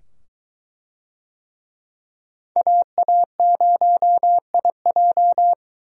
Morse code, heard as a keyed tone, about 22 wpm, 700 Hz. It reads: AA0IJ